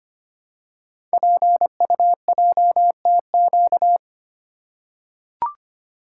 Morse code, heard as a keyed tone, about 25 wpm, 700 Hz. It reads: PUJTQ
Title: Code Groups - 5 Letter:Numbers - 25wpm